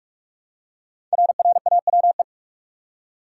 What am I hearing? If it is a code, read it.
Morse code, heard as a keyed tone, about 44 wpm, 690 Hz. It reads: RRAWE
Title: Code Groups - 5 Letters - 45wpm